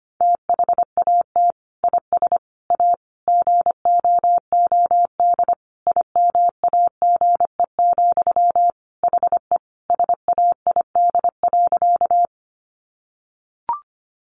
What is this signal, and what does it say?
Morse code, heard as a keyed tone, about 25 wpm, 700 Hz. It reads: THAT IS A GOOD IMAGE, HE SAID.
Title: Sentences from Top 700 Words 25wpm